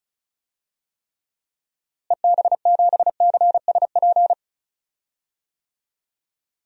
Morse code, heard as a keyed tone, about 36 wpm, 700 Hz. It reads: EB7CSP
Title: DXCC-100 Call Signs - 6 Characters - Repeated 3x - 35wpm